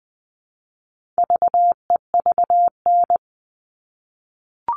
Morse code, heard as a keyed tone, about 20 wpm, 700 Hz. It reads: VEVN
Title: All Cut Numbers - 4-Digit 20wpm